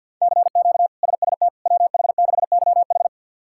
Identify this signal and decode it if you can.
Morse code, heard as a keyed tone, about 51 wpm, 700 Hz. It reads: CQ SST WH6YH
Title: K1USN SST QSO Practice - 50wpm